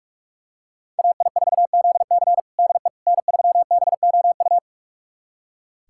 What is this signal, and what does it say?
Morse code, heard as a keyed tone, about 44 wpm, 700 Hz. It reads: AI4ZC DE N3BOU
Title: USA Callsign DE Callsign - 5 Characters - 45wpm